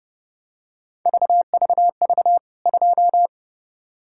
Morse code, heard as a keyed tone, about 30 wpm, 700 Hz. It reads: VVV 2
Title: K1USN SST QSO Practice 30wpm